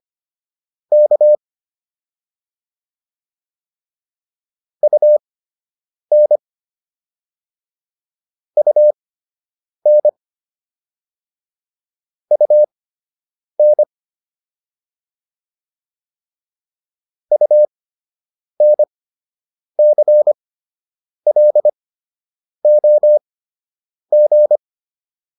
Morse code, heard as a keyed tone, about 25 wpm, spaced as for 8 wpm, 600 Hz. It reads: K UN UN UN UNCLOG